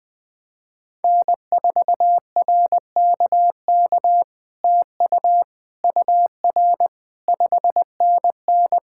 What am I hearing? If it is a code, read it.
Morse code, heard as a keyed tone, about 20 wpm, 700 Hz. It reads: N4RKK TU UR 5NN
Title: POTA QSO Practice - 20wpm